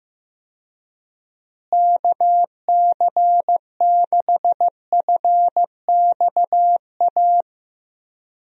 Morse code, heard as a keyed tone, about 16 wpm, 700 Hz. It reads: KC6FXA
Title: USA Call Signs - 6 Characters - Repeated 2x - 15wpm